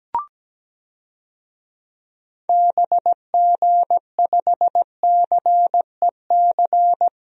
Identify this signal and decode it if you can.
Morse code, heard as a keyed tone, about 17 wpm, 700 Hz. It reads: BG5CEC
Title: DXCC-100 Call Signs - 6 Characters - Repeated 3x - 17wpm